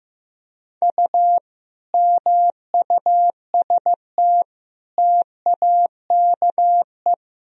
Morse code, heard as a keyed tone, about 16 wpm, 700 Hz. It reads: U MUST TAKE
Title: Sentences from Top 200 Words and Abbreviations 15wpm